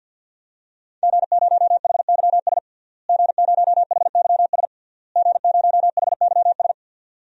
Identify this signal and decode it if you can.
Morse code, heard as a keyed tone, about 46 wpm, 700 Hz. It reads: G0HYS G0HYS G0HYS
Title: DXCC-100 Call Signs - 5 Characters - Repeated 3x - 50wpm